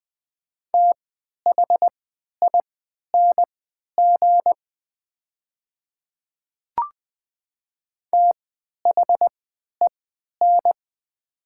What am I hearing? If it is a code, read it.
Morse code, heard as a keyed tone, about 20 wpm, spaced as for 11 wpm, 700 Hz. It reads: THING THEN
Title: 3x Character Spacing - Top 200 Words - 20wpm